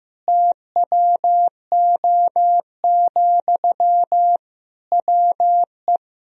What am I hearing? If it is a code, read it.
Morse code, heard as a keyed tone, about 16 wpm, 700 Hz. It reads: TWO, WE
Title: Sentences from Top 1000 Words - Review Part 1 - 15wpm